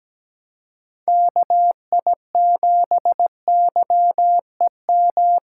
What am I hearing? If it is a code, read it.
Morse code, heard as a keyed tone, about 17 wpm, 700 Hz. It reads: KI7YEM
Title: USA Call Signs - 6 Characters - 3x Character Spacing - 17wpm